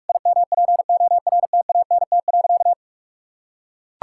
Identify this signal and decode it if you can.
Morse code, heard as a keyed tone, about 44 wpm, 700 Hz. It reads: IMPORTANT.
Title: Sentences from Top 1000 Words - Review Part 6 - 45wpm